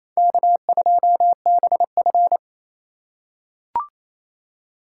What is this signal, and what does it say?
Morse code, heard as a keyed tone, about 28 wpm, 700 Hz. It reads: K2BF
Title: USA Call Signs - 4 Characters - 2x Character Spacing - 28wpm